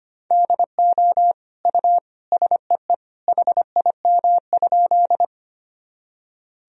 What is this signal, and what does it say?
Morse code, heard as a keyed tone, about 25 wpm, 700 Hz. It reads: DO U SEE HIM?
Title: Sentences from Top 100 Words and Abbreviations 25wpm